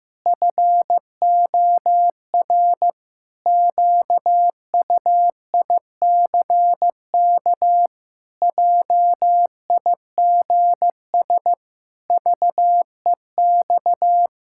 Morse code, heard as a keyed tone, about 15 wpm, 700 Hz. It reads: FOR QUICK JIGS VEX
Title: Pangram Challenge v2 - 15wpm